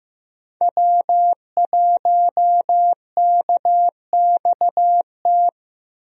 Morse code, heard as a keyed tone, about 15 wpm, 700 Hz. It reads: W1KXT
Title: USA Call Signs - 5 Characters - Repeated 2x - 15wpm